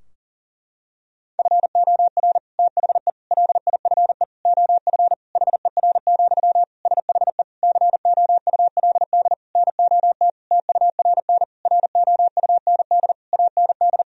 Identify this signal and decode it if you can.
Morse code, heard as a keyed tone, about 40 wpm, 700 Hz. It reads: FOR THE LIFE OF HER, SHE COULD NOT TURN ROUND AND
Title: Sentences from Top 1000 Words - Review Part 2 - 40wpm